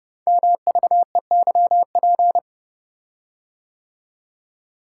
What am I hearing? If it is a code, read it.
Morse code, heard as a keyed tone, about 30 wpm, 700 Hz. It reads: MVEYP
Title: Code Groups - 5 Letters - 30wpm